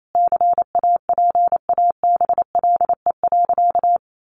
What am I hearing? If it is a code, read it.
Morse code, heard as a keyed tone, about 28 wpm, 700 Hz. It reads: CAPABLE.
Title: Sentences from Top 1200 Words 28wpm